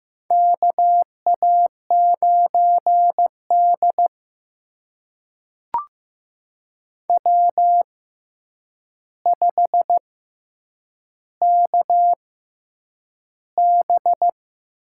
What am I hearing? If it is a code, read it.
Morse code, heard as a keyed tone, about 15 wpm, spaced as for 8 wpm, 700 Hz. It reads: KA9D W5KB
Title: USA Call Signs - 4 Characters - 6x Character Spacing - 15wpm